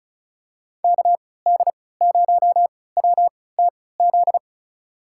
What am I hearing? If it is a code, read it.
Morse code, heard as a keyed tone, about 36 wpm, spaced as for 20 wpm, 700 Hz. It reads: KD0WTZ
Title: USA Call Signs - 6 Characters - 3x Character Spacing - 35wpm